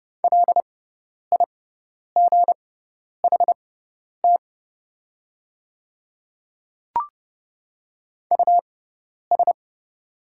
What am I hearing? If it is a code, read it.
Morse code, heard as a keyed tone, about 30 wpm, spaced as for 10 wpm, 700 Hz. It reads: LIGHT US
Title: 6x Character Spacing - Top 300 Words - 30wpm